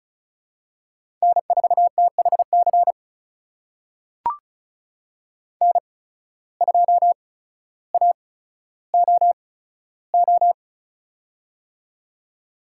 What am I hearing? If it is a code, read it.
Morse code, heard as a keyed tone, about 36 wpm, spaced as for 15 wpm, 700 Hz. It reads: N4THC N2AOO
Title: USA Call Signs - 5 Characters - 8x Character Spacing - 35wpm